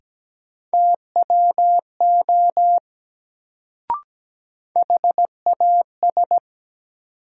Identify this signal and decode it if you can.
Morse code, heard as a keyed tone, about 17 wpm, 700 Hz. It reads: TWO HAS
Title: Sentences from Top 100 Words - New Word Drill - 17wpm